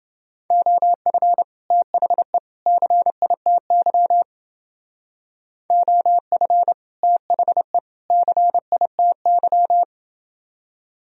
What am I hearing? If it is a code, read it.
Morse code, heard as a keyed tone, about 30 wpm, 700 Hz. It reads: OF THE CITY OF THE CITY